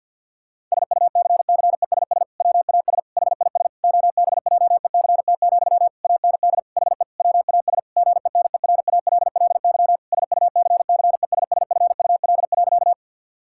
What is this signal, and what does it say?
Morse code, heard as a keyed tone, about 50 wpm, 700 Hz. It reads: SUCCESS WAS HIS OBJECT, AND HE WAS GENERALLY SUCCESSFUL.
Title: Sentences from Top 1000 Words - Review Part 6 - 50wpm